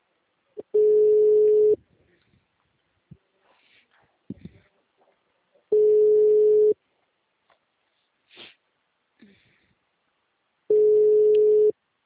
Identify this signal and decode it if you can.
Ringback tone (Europe, 425 Hz).